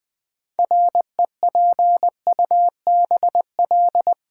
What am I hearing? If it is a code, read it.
Morse code, heard as a keyed tone, about 20 wpm, 690 Hz. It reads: REPUBL
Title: Build A Word - 3 Letters At A Time - 8-Letter Words - 20wpm